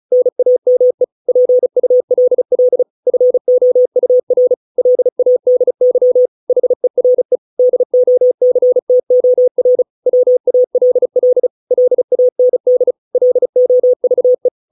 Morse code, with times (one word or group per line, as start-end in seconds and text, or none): 0.1-1.0 NAME
1.3-2.8 PULL
3.1-4.5 FOUR
4.8-6.3 LADY
6.5-7.4 HERE
7.6-9.8 DOCTOR
10.1-11.5 WALL
11.7-12.9 LAND
13.1-14.5 LOVE